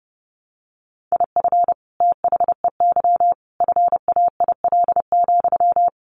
Morse code, heard as a keyed tone, about 30 wpm, 700 Hz. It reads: IF THEY FAIL,
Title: Sentences from Top 1100 Words 30wpm